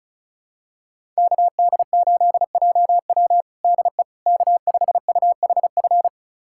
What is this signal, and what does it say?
Morse code, heard as a keyed tone, about 34 wpm, 700 Hz. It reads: KD8JW DE K5UHF
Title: USA Callsign DE Callsign - 5 Characters - 35wpm